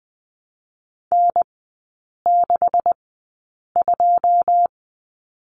Morse code, heard as a keyed tone, about 20 wpm, spaced as for 8 wpm, 700 Hz. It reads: N62
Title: 127 Lesson 21e - Number 6 - Three Characters 20wpm